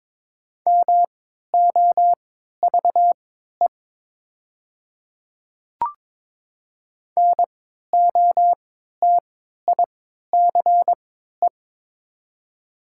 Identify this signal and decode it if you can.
Morse code, heard as a keyed tone, about 22 wpm, spaced as for 12 wpm, 700 Hz. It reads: MOVE NOTICE
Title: 3x Character Spacing - Top 500 Words - 22wpm